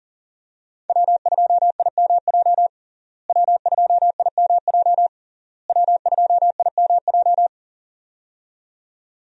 Morse code, heard as a keyed tone, about 40 wpm, 700 Hz. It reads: W2IMJ W2IMJ W2IMJ
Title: USA Call Signs - 5 Characters - Repeated 3x - 40wpm